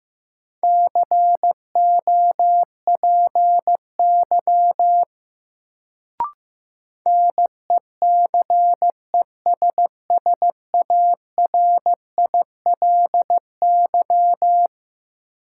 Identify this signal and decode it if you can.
Morse code, heard as a keyed tone, about 15 wpm, 700 Hz. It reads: COPY NECESSARILY
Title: Sentences from Top 1400 Words - New Word Drill Alt - 15wpm